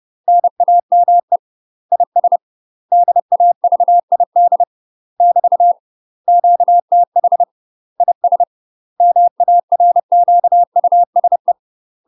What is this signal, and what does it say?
Morse code, heard as a keyed tone, about 30 wpm, 700 Hz. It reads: NAME IS DAVID <BT> QTH IS MARQUSE
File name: UK MASTER QSOs TRAINING / 30WPM / UK MASTER QSOs 0014